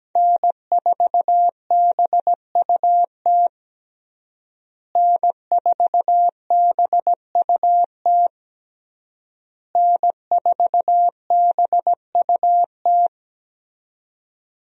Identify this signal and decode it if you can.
Morse code, heard as a keyed tone, about 17 wpm, 700 Hz. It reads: N4BUT N4BUT N4BUT
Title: USA Call Signs - 5 Characters - Repeated 3x - 17wpm